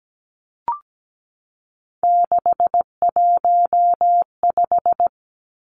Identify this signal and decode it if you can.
Morse code, heard as a keyed tone, about 17 wpm, 700 Hz. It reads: 615
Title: Numbers - 3-Digit 17wpm